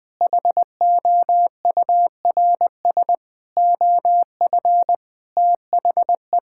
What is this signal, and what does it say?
Morse code, heard as a keyed tone, about 20 wpm, 700 Hz. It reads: HOURS OF THE
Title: Sentences with 7-Word Phrases - 20wpm